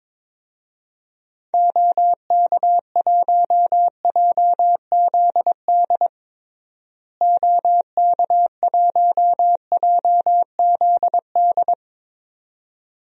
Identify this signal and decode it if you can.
Morse code, heard as a keyed tone, about 22 wpm, 700 Hz. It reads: OK1JZD OK1JZD
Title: DXCC-100 Call Signs - 6 Characters - Repeated 2x - 22wpm